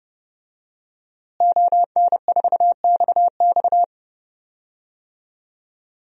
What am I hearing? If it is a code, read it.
Morse code, heard as a keyed tone, about 30 wpm, 700 Hz. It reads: ON4XX
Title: DXCC-100 Call Signs - 5 Characters - Repeated 2x - 30wpm